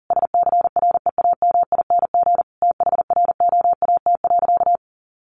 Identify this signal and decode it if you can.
Morse code, heard as a keyed tone, about 40 wpm, 700 Hz. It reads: SCREAMING THROAT.
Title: A Sound Of Thunder 40wpm